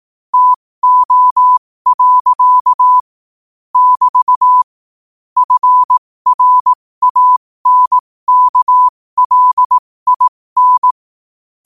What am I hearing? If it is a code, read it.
Morse code, heard as a keyed tone, about 18 wpm, spaced as for 15 wpm, 1000 Hz. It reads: TO. <BT> FRANKLIN
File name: qotdfiles / 15wpm